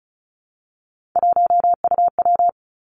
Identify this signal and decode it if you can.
Morse code, heard as a keyed tone, about 36 wpm, 700 Hz. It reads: 1UW
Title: Three Challenging Letter-Number Combinations 35wpm